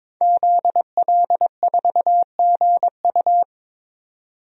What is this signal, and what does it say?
Morse code, heard as a keyed tone, about 21 wpm, 700 Hz. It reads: ZL4GU
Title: DXCC-100 Call Signs - 5 Characters - 22wpm